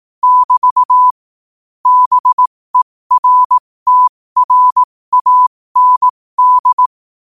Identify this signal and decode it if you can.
Morse code, heard as a keyed tone, about 18 wpm, spaced as for 15 wpm, 1000 Hz. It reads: <BT> BERTRAND
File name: qotdfiles / 15wpm